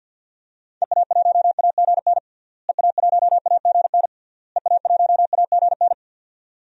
Morse code, heard as a keyed tone, about 50 wpm, 700 Hz. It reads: EA1AGN EA1AGN EA1AGN
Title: DXCC-100 Call Signs - 6 Characters - Repeated 3x - 50wpm